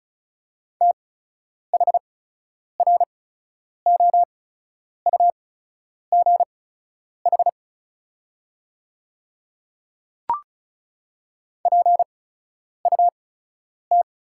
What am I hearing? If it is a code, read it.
Morse code, heard as a keyed tone, about 36 wpm, spaced as for 10 wpm, 700 Hz. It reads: THROUGH PUT
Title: 8x Character Spacing - Top 500 Words - 35wpm